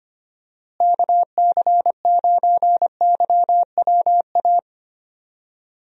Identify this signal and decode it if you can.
Morse code, heard as a keyed tone, about 25 wpm, 700 Hz. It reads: KC9YWA
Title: USA Call Signs - 6 Characters - Repeated 2x - 25wpm